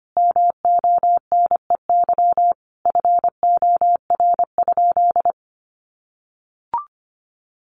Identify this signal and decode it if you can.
Morse code, heard as a keyed tone, about 25 wpm, 700 Hz. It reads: MONEY FOR?
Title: Sentences from Top 400 Words 25wpm